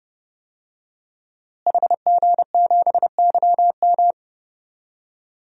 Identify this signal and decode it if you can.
Morse code, heard as a keyed tone, about 30 wpm, 700 Hz. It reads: HG7YM
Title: DXCC-100 Call Signs - 5 Characters - Repeated 2x - 30wpm